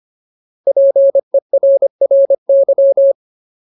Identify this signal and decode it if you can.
Morse code, heard as a keyed tone, about 25 wpm, 550 Hz. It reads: PERRY
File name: Ultimate Word List / 25WPM Ultimate